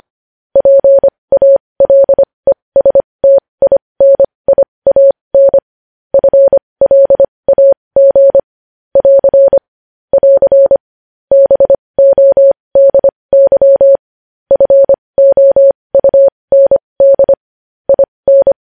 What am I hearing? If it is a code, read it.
Morse code, heard as a keyed tone, about 25 wpm, spaced as for 20 wpm, 550 Hz. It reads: PALESTINIAN FLAG <AR> <AR> BODY FOUND IN
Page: news-headlines-in-morse-code-at-20-wpm